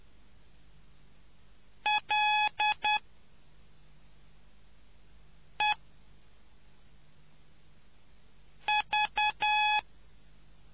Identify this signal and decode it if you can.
Morse code, heard as a keyed tone, about 10 wpm, spaced as for 3 wpm, 850 Hz. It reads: LEV